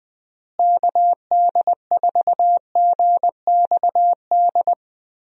Morse code, heard as a keyed tone, about 20 wpm, 700 Hz. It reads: KD4GXD
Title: USA Call Signs - 6 Characters - Repeated 2x - 20wpm